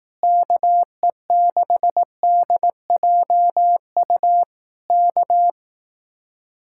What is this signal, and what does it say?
Morse code, heard as a keyed tone, about 18 wpm, 700 Hz. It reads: KE6DJU K
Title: Speed-Racing Common QSO Phrases Encore - 15wpm